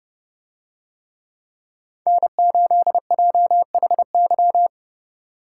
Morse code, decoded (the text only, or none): N8JHY